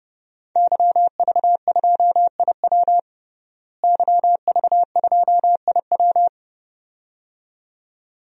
Morse code, heard as a keyed tone, about 32 wpm, 700 Hz. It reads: YV2IW YV2IW
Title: DXCC-100 Call Signs - 5 Characters - Repeated 2x - 30wpm